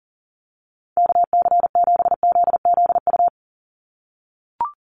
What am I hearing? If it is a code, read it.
Morse code, heard as a keyed tone, about 38 wpm, 700 Hz. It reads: KC7ZZU
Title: USA Call Signs 40wpm